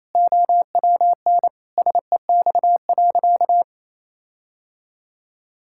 Morse code, heard as a keyed tone, about 28 wpm, 700 Hz. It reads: OWN SEX.
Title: Sentences from Top 1600 Words 28wpm